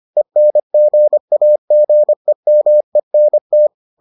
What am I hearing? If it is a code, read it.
Morse code, heard as a keyed tone, about 25 wpm, 600 Hz. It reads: ENGAGEMENT